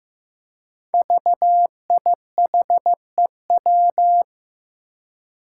Morse code, heard as a keyed tone, about 15 wpm, 700 Hz. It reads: VIHEW
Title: Code Groups - 5 Letter:Numbers - 15wpm